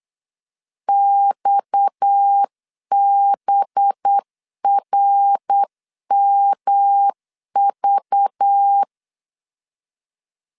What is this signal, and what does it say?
Morse code, heard as a keyed tone, about 8 wpm, 800 Hz. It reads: XBRMV
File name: LEZIONI / 7 GF9Z / 7 G 8wpm 3118